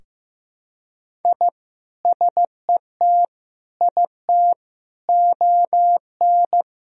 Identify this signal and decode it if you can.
Morse code, heard as a keyed tone, about 15 wpm, 700 Hz. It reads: I SET IT ON
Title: Sentences from Top 1000 Words - Review Part 2 - 15wpm